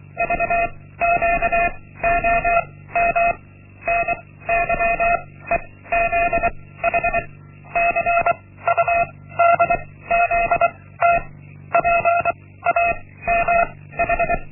Morse code, heard as a keyed tone, about 23 wpm, 690 Hz. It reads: V Q O M N Y E Z H C U D Z T P A M H
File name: Linksmakalnis-1